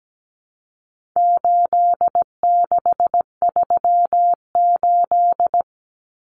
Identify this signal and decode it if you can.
Morse code, heard as a keyed tone, about 17 wpm, 700 Hz. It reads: 8638
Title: Common Cut Numbers - 4-Digit 17wpm